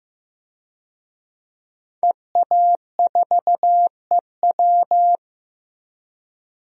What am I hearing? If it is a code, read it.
Morse code, heard as a keyed tone, about 15 wpm, 700 Hz. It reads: EA4EW
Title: DXCC-100 Call Signs - 5 Characters - Repeated 2x - 15wpm